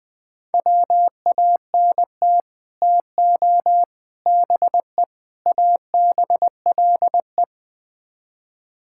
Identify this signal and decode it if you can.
Morse code, heard as a keyed tone, about 21 wpm, 700 Hz. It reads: WANT TO BE ABLE
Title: Build A Phrase - 5-Word Phrases - 20wpm